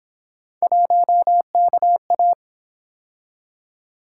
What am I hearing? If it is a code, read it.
Morse code, heard as a keyed tone, about 29 wpm, 700 Hz. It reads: 1KA